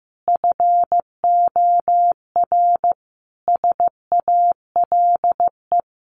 Morse code, heard as a keyed tone, about 15 wpm, 700 Hz. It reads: FOR SALE
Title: Sentences from Top 1400 Words 15wpm